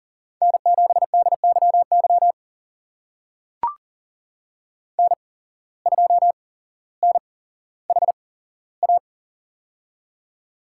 Morse code, heard as a keyed tone, about 40 wpm, spaced as for 17 wpm, 700 Hz. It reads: N7DYY N2NHA
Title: USA Call Signs - 5 Characters - 8x Character Spacing - 40wpm